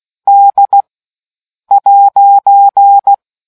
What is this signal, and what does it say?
Morse code, heard as a keyed tone, about 16 wpm, spaced as for 7 wpm, 800 Hz. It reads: D'